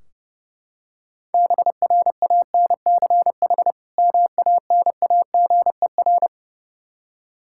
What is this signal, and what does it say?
Morse code, heard as a keyed tone, about 32 wpm, 700 Hz. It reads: BRANCH MANAGER
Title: K1USN SST QSO Practice 30wpm